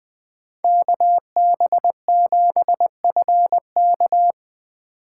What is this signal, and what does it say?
Morse code, heard as a keyed tone, about 20 wpm, 700 Hz. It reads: KB7FK
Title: USA Call Signs - 5 Characters - 2x Character Spacing - 20wpm